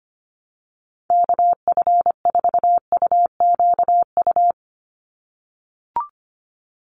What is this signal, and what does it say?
Morse code, heard as a keyed tone, about 25 wpm, 700 Hz. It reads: KF4UQU
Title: USA Call Signs 25wpm